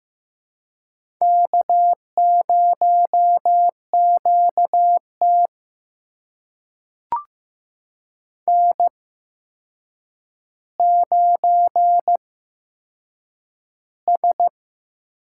Morse code, heard as a keyed tone, about 15 wpm, spaced as for 7 wpm, 700 Hz. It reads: K0QT N9S
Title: USA Call Signs - 4 Characters - 8x Character Spacing - 15wpm